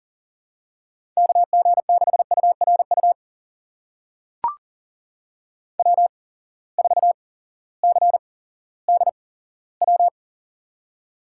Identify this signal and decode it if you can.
Morse code, heard as a keyed tone, about 40 wpm, spaced as for 18 wpm, 700 Hz. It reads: KG6URU W4CDW
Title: USA Call Signs - 8x Character Spacing - 40wpm